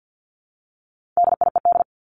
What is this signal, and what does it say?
Morse code, heard as a keyed tone, about 50 wpm, 700 Hz. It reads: DIED
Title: ED Suffix - Top 100 Words with Base Words 50wpm